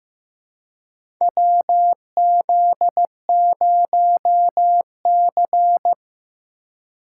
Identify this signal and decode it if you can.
Morse code, heard as a keyed tone, about 15 wpm, 700 Hz. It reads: WZ0C
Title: USA Call Signs - 2x Character Spacing - 15wpm